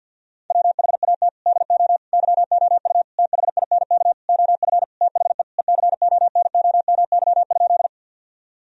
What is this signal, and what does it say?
Morse code, heard as a keyed tone, about 50 wpm, 700 Hz. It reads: WHAT DO YOU THINK OF THE ECONOMY?
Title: Sentences from Top 1000 Words - Review Part 3 - 50wpm